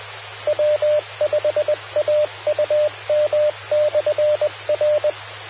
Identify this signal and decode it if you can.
Morse code, heard as a keyed tone, about 20 wpm, 600 Hz. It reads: W5AUM/R
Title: Austin, TX Amateur Repeaters